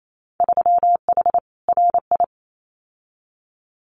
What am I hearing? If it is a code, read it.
Morse code, heard as a keyed tone, about 28 wpm, 700 Hz. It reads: 3H RI
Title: Winter Field Day Exchanges Encore 28wpm